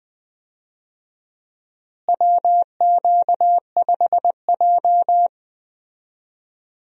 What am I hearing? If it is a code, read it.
Morse code, heard as a keyed tone, about 21 wpm, 700 Hz. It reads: WQ5J
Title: USA Call Signs - 4 Characters - Repeated 2x - 20wpm